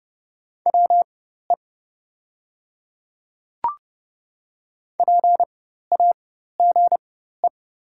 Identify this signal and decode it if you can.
Morse code, heard as a keyed tone, about 30 wpm, spaced as for 14 wpm, 690 Hz. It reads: WE PAGE
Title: 4x Character Spacing - Top 500 Words - 30wpm